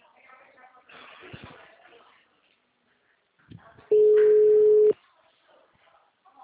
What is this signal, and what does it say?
Ringback tone (Europe, 425 Hz), with speech.